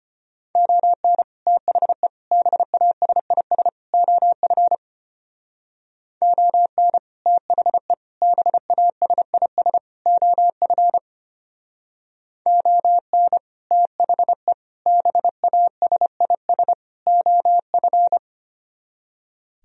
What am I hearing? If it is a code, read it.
Morse code, heard as a keyed tone, about 32 wpm, 700 Hz. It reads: ON THE BASIS OF ON THE BASIS OF ON THE BASIS OF